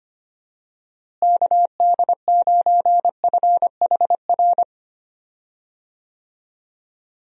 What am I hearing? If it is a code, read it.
Morse code, heard as a keyed tone, about 25 wpm, 700 Hz. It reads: KD9FHR
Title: USA Call Signs - 6 Characters - Repeated 2x - 25wpm